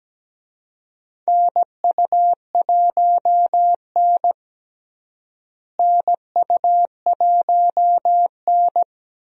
Morse code, heard as a keyed tone, about 17 wpm, 700 Hz. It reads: NU1N NU1N
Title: USA Call Signs - 4 Characters - Repeated 3x - 17wpm